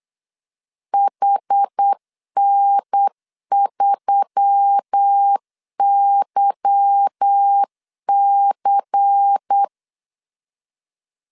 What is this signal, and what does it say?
Morse code, heard as a keyed tone, about 8 wpm, 800 Hz. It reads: HN3YC
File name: CW / DET / 1 9 1500C S A 8wpm 5637